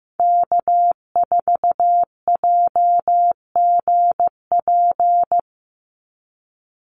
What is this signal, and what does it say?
Morse code, heard as a keyed tone, about 15 wpm, 700 Hz. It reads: K4JGP
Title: USA Call Signs 15wpm